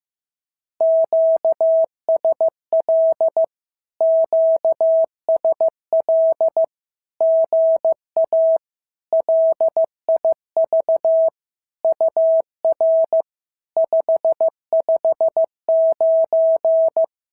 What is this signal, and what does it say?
Morse code, heard as a keyed tone, about 15 wpm, 650 Hz. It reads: QSL QSL GA LIV UR 559
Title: SOTA QSO Practice 15wpm